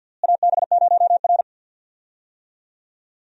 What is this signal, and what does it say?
Morse code, heard as a keyed tone, about 50 wpm, 700 Hz. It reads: AB0R
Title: USA Call Signs - 4 Characters - Repeated 2x - 50wpm